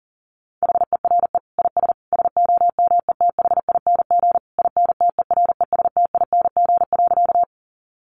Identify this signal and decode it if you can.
Morse code, heard as a keyed tone, about 40 wpm, 700 Hz. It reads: HERE IS SOMETHING INTERESTING.
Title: Sentences from Top 1000 Words 40wpm